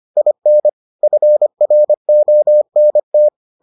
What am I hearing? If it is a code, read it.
Morse code, heard as a keyed tone, about 25 wpm, 600 Hz. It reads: IN FRONT